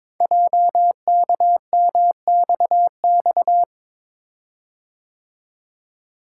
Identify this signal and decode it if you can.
Morse code, heard as a keyed tone, about 22 wpm, 700 Hz. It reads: JKMXX
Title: Code Groups - 5 Letters - 22wpm